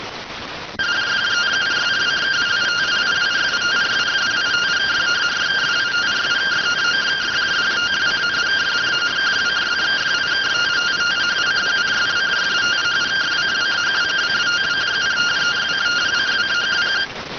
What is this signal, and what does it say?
Two-tone teleprinter signal, about 170 Hz shift.